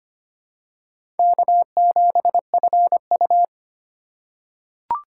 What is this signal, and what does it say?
Morse code, heard as a keyed tone, about 25 wpm, 700 Hz. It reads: K7FU
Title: USA Call Signs - 4 Characters - 2x Character Spacing - 25wpm